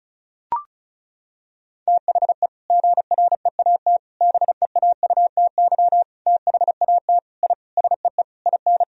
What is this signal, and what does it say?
Morse code, heard as a keyed tone, about 34 wpm, 700 Hz. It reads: THE GREAT BEAUTY THAT I SEE IN
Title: Sentences from Top 1000 Words - Review Part 2 - 35wpm